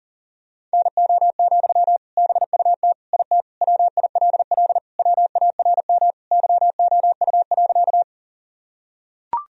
Morse code, heard as a keyed tone, about 37 wpm, 700 Hz. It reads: NO, BUT IT WILL WARM YOU.
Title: Sentences from Top 1000 Words - Review Part 2 - 40wpm